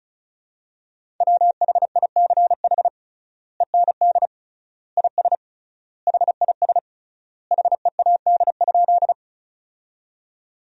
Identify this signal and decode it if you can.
Morse code, heard as a keyed tone, about 34 wpm, 700 Hz. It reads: WHICH END IS HIS HEAD?
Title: 3x Word-Spacing - Sentences from Top 300 Words - 35wpm